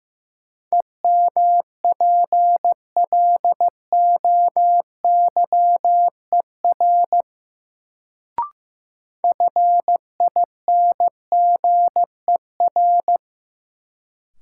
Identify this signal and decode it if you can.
Morse code, heard as a keyed tone, about 15 wpm, 700 Hz. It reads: EMPLOYER FINGER
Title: Sentences from Top 1600 Words - New Word Drill Alt - 15wpm